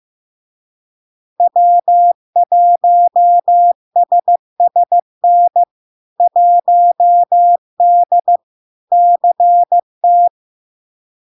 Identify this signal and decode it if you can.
Morse code, heard as a keyed tone, about 15 wpm, 700 Hz. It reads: W1SSN 1D CT
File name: LARK-FD-2022-Log-in-CW-15wpm-700-Hz